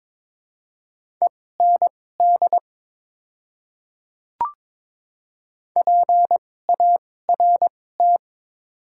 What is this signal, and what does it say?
Morse code, heard as a keyed tone, about 22 wpm, spaced as for 16 wpm, 700 Hz. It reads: END PART
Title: 2x Character Spacing - Top 200 Words - 22wpm